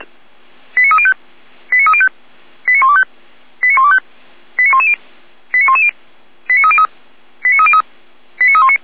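Five-tone selective call (ZVEI), heard as quick runs of five tones, tones of about 80 ms. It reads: 89305 89305 89125 89125 89110 89110 89303 89303 89319
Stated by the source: ZVEI five-tone selective call received by SDR.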